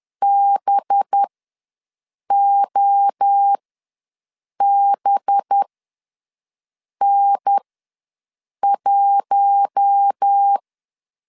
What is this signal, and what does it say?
Morse code, heard as a keyed tone, about 10 wpm, spaced as for 6 wpm, 800 Hz. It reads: BOBN1
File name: LEZIONI / 5 LBVC / 5 LBV ws 10wpm 3645